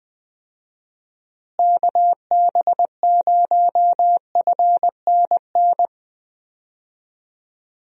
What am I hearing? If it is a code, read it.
Morse code, heard as a keyed tone, about 20 wpm, 700 Hz. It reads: KB0FNN
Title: USA Call Signs - 6 Characters - Repeated 2x - 20wpm